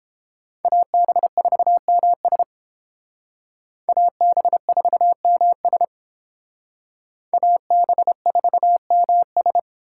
Morse code, heard as a keyed tone, about 29 wpm, 700 Hz. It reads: AB4MS AB4MS AB4MS